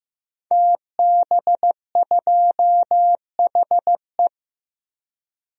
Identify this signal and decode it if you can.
Morse code, heard as a keyed tone, about 15 wpm, 700 Hz. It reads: TB2HE
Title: DXCC-100 Call Signs - 5 Characters - Repeated 3x - 15wpm